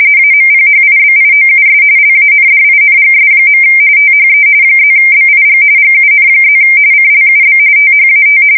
RTTY, about 45.45 baud. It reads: AMPHIBIOUS LANDINGS. THE +BIG GUN+ CLUB THAT
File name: fsk170 / usnavy4